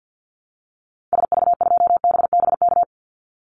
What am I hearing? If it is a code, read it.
Morse code, heard as a keyed tone, about 50 wpm, 700 Hz. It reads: SV2BBK
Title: Call Signs Repeated 2x - 50wpm